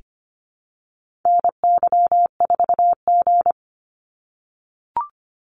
Morse code, heard as a keyed tone, about 25 wpm, 700 Hz. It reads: NY4G
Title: NA SOTA Activator Callsigns 25wpm